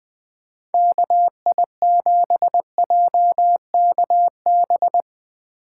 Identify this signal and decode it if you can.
Morse code, heard as a keyed tone, about 22 wpm, 700 Hz. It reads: KI7JKB
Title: Speed Racing - USA 2x3 Call Signs 20wpm